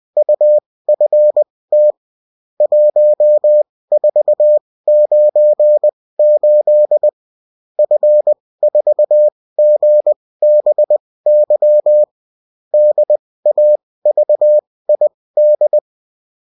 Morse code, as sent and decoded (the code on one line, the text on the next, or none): ..- ..-. - / .---- ....- ----. ---.. / ..-. ....- --. -... -.-- / -.. .- ...- .. -..
UFT 1498 F4GBY DAVID